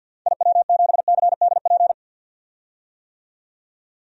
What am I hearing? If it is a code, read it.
Morse code, heard as a keyed tone, about 53 wpm, 700 Hz. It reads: IW7CDP
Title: DXCC-100 Call Signs - 6 Characters - 50wpm